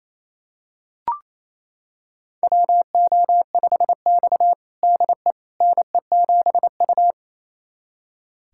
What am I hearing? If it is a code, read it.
Morse code, heard as a keyed tone, about 28 wpm, 700 Hz. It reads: WO5X DE NE7U
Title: USA Callsign DE Callsign - 4 Characters - 28wpm